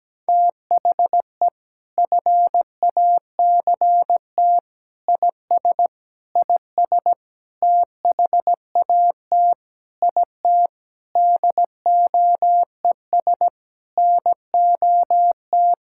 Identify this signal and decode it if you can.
Morse code, heard as a keyed tone, about 17 wpm, 700 Hz. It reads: THE FACT IS IS THAT IT DOES NOT
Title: Sentences from Top 1600 Words 17wpm